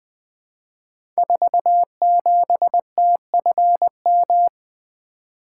Morse code, heard as a keyed tone, about 20 wpm, 700 Hz. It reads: 47TFM
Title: Code Groups - 5 Letter:Numbers - 20wpm